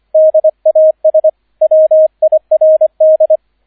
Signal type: Morse code